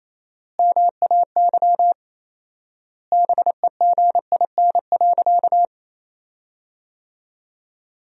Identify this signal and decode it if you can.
Morse code, heard as a keyed tone, about 28 wpm, 700 Hz. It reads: MAY BEGIN.
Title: 4x Word-Spacing - Sentences from Top 300 Words - 28wpm